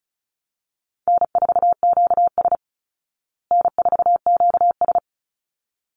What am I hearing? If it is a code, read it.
Morse code, heard as a keyed tone, about 34 wpm, 700 Hz. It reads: N4QS N4QS
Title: Easy Call Signs Repeated 2x - 35wpm